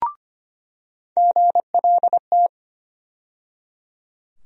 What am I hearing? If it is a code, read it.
Morse code, heard as a keyed tone, about 25 wpm, 700 Hz. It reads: GLT
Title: USA Call Sign Postfixes - 3 Letters 25wpm